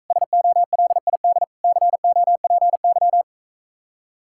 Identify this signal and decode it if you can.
Morse code, heard as a keyed tone, about 45 wpm, 700 Hz. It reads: SOLID COPY